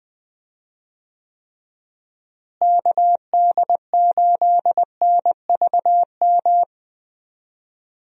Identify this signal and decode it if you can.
Morse code, heard as a keyed tone, about 21 wpm, 700 Hz. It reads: KD8NVM
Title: USA Call Signs - 6 Characters - Repeated 2x - 20wpm